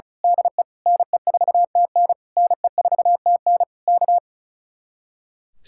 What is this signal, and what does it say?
Morse code, heard as a keyed tone, about 32 wpm, 700 Hz. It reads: DE NE4TN NE4TN K
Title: Speed-Racing Common QSO Phrases Encore - 35wpm